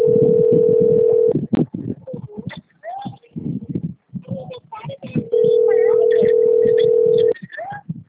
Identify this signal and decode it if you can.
Ringback tone (North America), 440 + 480 Hz.